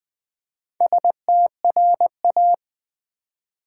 Morse code, heard as a keyed tone, about 20 wpm, 700 Hz. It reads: STRA
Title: Build A Word - 2 Letters At A Time - 8-Letter Words - 20wpm